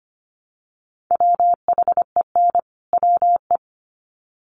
Morse code, heard as a keyed tone, about 25 wpm, 700 Hz. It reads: WHEN WE
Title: Sets of 2 Words 25wpm